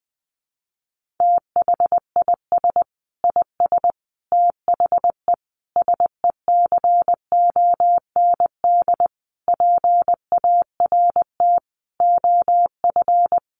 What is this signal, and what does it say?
Morse code, heard as a keyed tone, about 20 wpm, 700 Hz. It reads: THIS IS THE SECOND PART OF
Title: Sentences from Top 1400 Words 20wpm